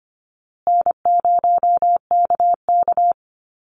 Morse code, heard as a keyed tone, about 25 wpm, 700 Hz. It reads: N0KK
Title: Easy Call Signs Repeated 3x - 25wpm